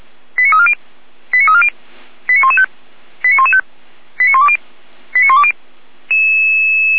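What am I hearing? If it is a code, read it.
Five-tone selective call (ZVEI), heard as quick runs of five tones, tones of about 80 ms. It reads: 89340 89340 89115 89115 89120 89120
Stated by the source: ZVEI five-tone selective call received by SDR.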